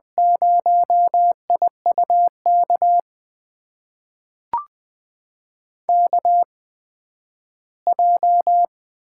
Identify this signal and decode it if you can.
Morse code, heard as a keyed tone, about 20 wpm, spaced as for 12 wpm, 700 Hz. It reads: 0IUK KJ
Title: USA Call Signs - 8x Character Spacing - 20wpm